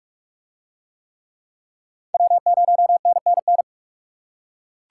Morse code, heard as a keyed tone, about 37 wpm, 700 Hz. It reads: W0NNN
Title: Speed Racing - USA 1x3 Call Signs 30wpm